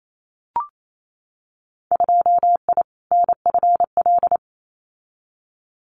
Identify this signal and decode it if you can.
Morse code, heard as a keyed tone, about 28 wpm, 700 Hz. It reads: 2I NFL
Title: Winter Field Day Exchanges 28wpm